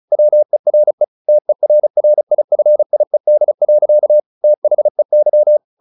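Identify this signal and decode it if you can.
Morse code, heard as a keyed tone, about 34 wpm, 600 Hz. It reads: WERE TERRIFIED. THEY
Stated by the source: Morse practice at 35 wpm from CW med Gnister, sending Ken Follett's novel Never in English.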